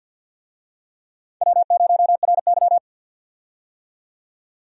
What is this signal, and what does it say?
Morse code, heard as a keyed tone, about 46 wpm, 700 Hz. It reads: W0RY
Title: USA Call Signs - 4 Characters - 50wpm